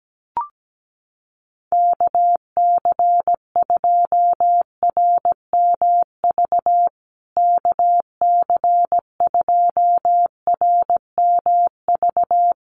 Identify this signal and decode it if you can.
Morse code, heard as a keyed tone, about 17 wpm, 700 Hz. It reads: KC2RMV KC2RMV
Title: USA Call Signs Repeated 3x 17wpm